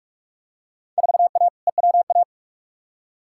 Morse code, heard as a keyed tone, about 41 wpm, 700 Hz. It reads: 4A EWA
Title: Field Day Exchanges Encore - 45wpm